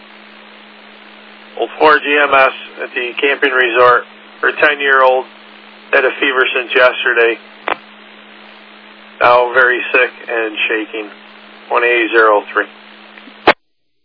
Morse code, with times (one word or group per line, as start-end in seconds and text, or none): none